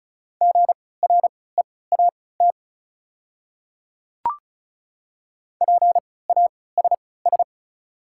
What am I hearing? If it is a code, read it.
Morse code, heard as a keyed tone, about 34 wpm, spaced as for 20 wpm, 690 Hz. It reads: GREAT PASS
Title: 3x Character Spacing - Top 500 Words - 35wpm